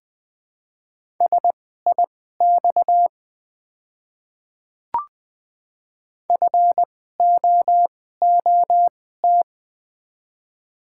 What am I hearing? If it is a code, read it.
Morse code, heard as a keyed tone, about 21 wpm, spaced as for 15 wpm, 700 Hz. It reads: SIX FOOT
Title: 2x Character Spacing - Top 500 Words - 20wpm